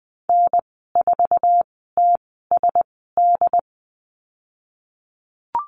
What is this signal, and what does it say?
Morse code, heard as a keyed tone, about 20 wpm, spaced as for 15 wpm, 700 Hz. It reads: N4TSD
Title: 088 Lesson 15g - Number 5 - Call Signs 20wpm